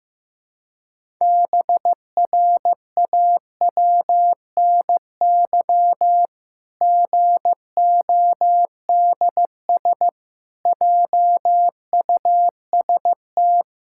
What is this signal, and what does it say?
Morse code, heard as a keyed tone, about 15 wpm, 700 Hz. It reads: BRAWNY GODS JUST
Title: Pangram Challenge v2 - 15wpm